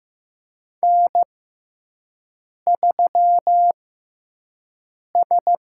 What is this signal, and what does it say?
Morse code, heard as a keyed tone, about 15 wpm, spaced as for 5 wpm, 700 Hz. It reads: N3S
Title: USA Call Signs - 6x Character Spacing - 15wpm